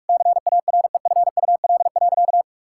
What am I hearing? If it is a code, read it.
Morse code, heard as a keyed tone, about 41 wpm, 700 Hz. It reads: CAREFUL.
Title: Sentences from Top 1600 Words 45wpm